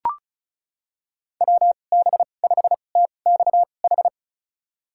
Morse code, heard as a keyed tone, about 36 wpm, spaced as for 25 wpm, 700 Hz. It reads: WB5TXH
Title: USA Call Signs - 2x Character Spacing - 35wpm